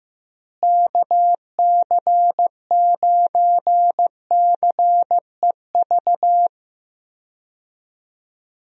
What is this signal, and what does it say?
Morse code, heard as a keyed tone, about 15 wpm, 700 Hz. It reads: KC9CEV
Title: USA Call Signs - 6 Characters - Repeated 2x - 15wpm